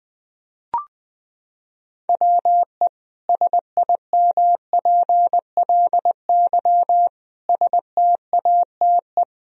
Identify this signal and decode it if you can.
Morse code, heard as a keyed tone, about 21 wpm, 700 Hz. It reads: WE SIMPLY STATE
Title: Sentences from Top 1000 Words - Review Part 4 - 20wpm